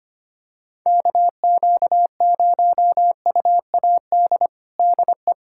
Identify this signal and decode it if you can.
Morse code, heard as a keyed tone, about 25 wpm, 700 Hz. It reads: KQ0UAD DE
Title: USA Callsign DE Callsign 25wpm